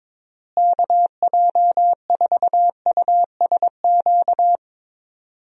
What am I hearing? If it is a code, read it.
Morse code, heard as a keyed tone, about 22 wpm, 700 Hz. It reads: KJ4USQ
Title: USA Call Signs - 6 Characters - Repeated 2x - 22wpm